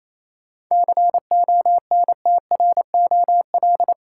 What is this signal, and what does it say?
Morse code, heard as a keyed tone, about 28 wpm, 700 Hz. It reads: CONTROL
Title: Sentences from Top 1000 Words - Review Part 2 - New Word Drill - 28wpm